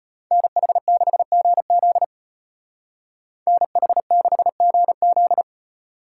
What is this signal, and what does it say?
Morse code, heard as a keyed tone, about 32 wpm, 700 Hz. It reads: NH6GZ NH6GZ